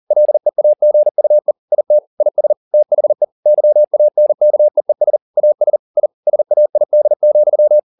Morse code, heard as a keyed tone, about 40 wpm, 600 Hz. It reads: LEAGUE IT IS THE YANKEES AS I SAID,